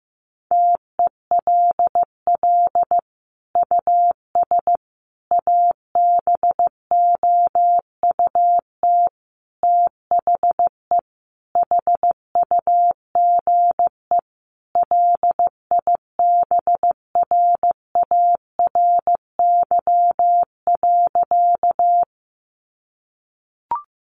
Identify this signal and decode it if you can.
Morse code, heard as a keyed tone, about 14 wpm, 700 Hz. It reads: TELL US ABOUT THE HUGE LIBRARY.
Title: Sentences from Top 1100 Words 15wpm